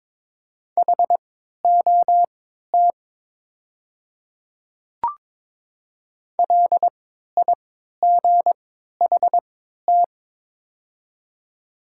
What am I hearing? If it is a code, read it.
Morse code, heard as a keyed tone, about 22 wpm, spaced as for 12 wpm, 700 Hz. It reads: HOT LIGHT
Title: 3x Character Spacing - Top 200 Words - 22wpm